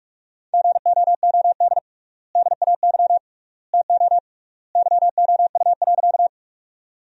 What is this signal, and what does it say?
Morse code, heard as a keyed tone, about 46 wpm, 700 Hz. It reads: GOOD DAY TO YOU.
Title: 3x Word-Spacing - Sentences from Top 200 Words - 45wpm